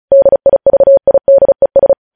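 Morse code, heard as a keyed tone, about 36 wpm, 550 Hz. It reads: DIVIDES
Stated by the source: Morse practice sending news headlines at 35 wpm.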